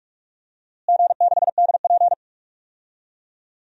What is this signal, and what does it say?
Morse code, heard as a keyed tone, about 46 wpm, 700 Hz. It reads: G6DP
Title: DXCC-100 Call Signs - 4 Characters - Repeated 3x - 45wpm